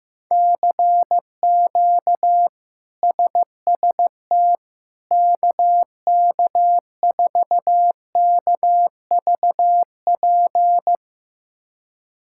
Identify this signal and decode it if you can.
Morse code, heard as a keyed tone, about 15 wpm, 700 Hz. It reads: CQ SST KK4KVP
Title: K1USN SST QSO Practice - 15wpm